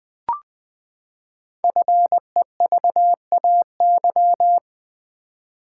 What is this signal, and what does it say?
Morse code, heard as a keyed tone, about 20 wpm, 700 Hz. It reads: FEVAY
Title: Code Groups - 5 Letter:Numbers - 20wpm